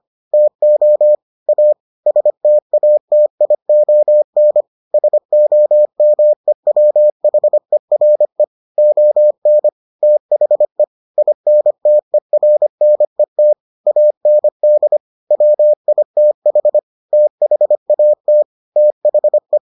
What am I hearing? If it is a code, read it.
Morse code, heard as a keyed tone, about 25 wpm, 600 Hz. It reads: TO A STATION SOMEWHERE ON THE INTERNET AND WITH THAT THE